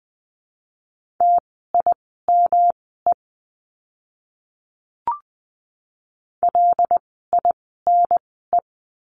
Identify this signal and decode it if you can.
Morse code, heard as a keyed tone, about 20 wpm, spaced as for 14 wpm, 700 Hz. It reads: TIME LINE
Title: 115 Lesson 19f - Letter w - Words 20wpm